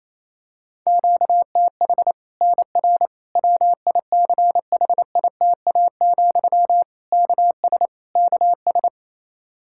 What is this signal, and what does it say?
Morse code, heard as a keyed tone, about 28 wpm, 700 Hz. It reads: QTH NR WICHITA, KS KS
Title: Practice QSO - Exchange 1 - Beginning - 28wpm